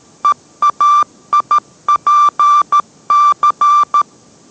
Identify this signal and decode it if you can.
Morse code, heard as a keyed tone, about 13 wpm, 1180 Hz. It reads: EAIPC